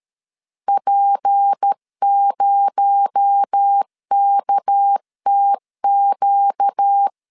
Morse code, heard as a keyed tone, about 13 wpm, 800 Hz. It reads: P0KTQ
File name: CW / DET / 1 6 1500C S A 12wpm 3440